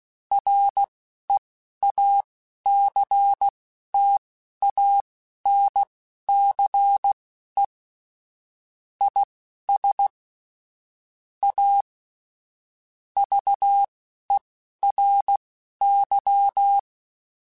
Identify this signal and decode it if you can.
Morse code, heard as a keyed tone, about 16 wpm, spaced as for 11 wpm, 800 Hz. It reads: REACTANCE IS A VERY
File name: Technical Subjects / Technical Subjects-mp3s / Reactance @10